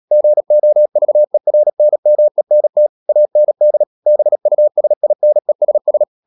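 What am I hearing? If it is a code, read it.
Morse code, heard as a keyed tone, about 37 wpm, 600 Hz. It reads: GOVERNMENT AND BUSINESS
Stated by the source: Morse code practice at 35–39 wpm sending English text of Ken Follett's novel Never, from CW med Gnister.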